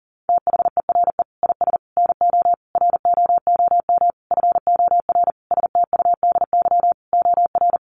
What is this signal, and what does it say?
Morse code, heard as a keyed tone, about 40 wpm, 700 Hz. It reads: THERE IS NO ROOM FOR STUDY OR
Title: Sentences from Top 1200 Words 40wpm